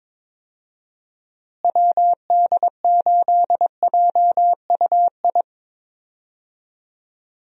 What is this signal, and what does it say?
Morse code, heard as a keyed tone, about 21 wpm, 700 Hz. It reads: WD8JUI
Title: USA Call Signs - 6 Characters - Repeated 3x - 22wpm